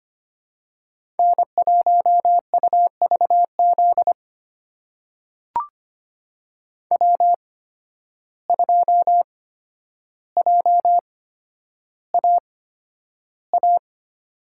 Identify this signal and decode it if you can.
Morse code, heard as a keyed tone, about 25 wpm, spaced as for 11 wpm, 700 Hz. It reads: N1UVZ W2JAA
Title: USA Call Signs - 5 Characters - 8x Character Spacing - 25wpm